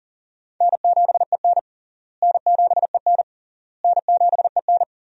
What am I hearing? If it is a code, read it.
Morse code, heard as a keyed tone, about 40 wpm, 700 Hz. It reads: N7EN N7EN N7EN
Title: USA Call Signs - 4 Characters - Repeated 3x - 40wpm